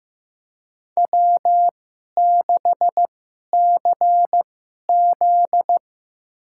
Morse code, heard as a keyed tone, about 15 wpm, spaced as for 11 wpm, 700 Hz. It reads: W6CZ
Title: USA Call Signs - 4 Characters - 2x Character Spacing - 15wpm